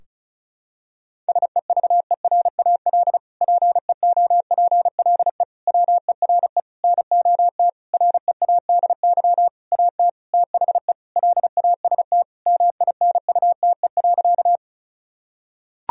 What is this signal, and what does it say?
Morse code, heard as a keyed tone, about 34 wpm, 700 Hz. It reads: SEVERAL PEOPLE WERE NOT READY AT THE LAST MINUTE.
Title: Sentences from Top 1000 Words - Review Part 2 - 35wpm